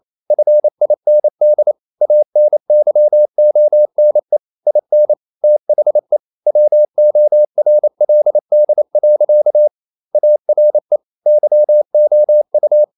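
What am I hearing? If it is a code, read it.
Morse code, heard as a keyed tone, about 28 wpm, 600 Hz. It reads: FIND ANYONE IN THE WORLD. ARE YOU
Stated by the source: Morse code practice at 28 wpm, text of Ken Follett's novel Never in English, from CW med Gnister.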